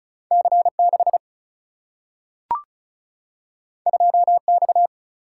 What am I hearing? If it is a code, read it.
Morse code, heard as a keyed tone, about 36 wpm, 700 Hz. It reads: C 6 2 X
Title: Single-Letter Single-Number 35wpm